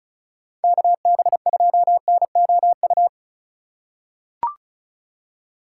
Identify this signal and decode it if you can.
Morse code, heard as a keyed tone, about 34 wpm, 700 Hz. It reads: KB2NOU
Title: USA Call Signs - 3x Character Spacing - 35wpm